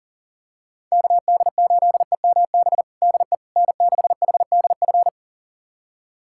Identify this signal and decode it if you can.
Morse code, heard as a keyed tone, about 37 wpm, 700 Hz. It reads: KD8EMB DE N6HDF
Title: USA Callsign DE Callsign 40wpm